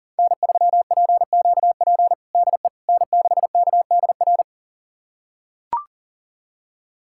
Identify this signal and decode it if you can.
Morse code, heard as a keyed tone, about 40 wpm, 700 Hz. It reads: N3PQP DE N6KDR
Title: USA Callsign DE Callsign - 5 Characters - 40wpm